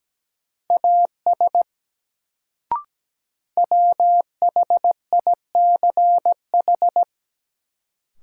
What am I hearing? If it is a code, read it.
Morse code, heard as a keyed tone, about 17 wpm, 700 Hz. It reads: AS WHICH
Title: Sets of 2 Words - New Word Drill - 17wpm